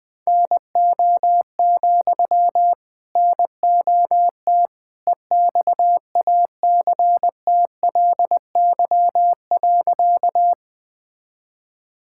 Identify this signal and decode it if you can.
Morse code, heard as a keyed tone, about 20 wpm, 700 Hz. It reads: NO, NOT EXACTLY.
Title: Sentences from Top 1000 Words - Review Part 6 - 20wpm